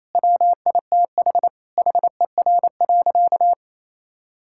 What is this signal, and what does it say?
Morse code, heard as a keyed tone, about 28 wpm, 700 Hz. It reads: WITH HER.
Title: Sentences from Top 1000 Words - Review Part 5 - 28wpm